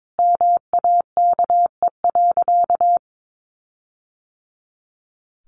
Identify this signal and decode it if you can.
Morse code, heard as a keyed tone, about 22 wpm, 700 Hz. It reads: MAKE.
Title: Sentences from Top 1400 Words 22wpm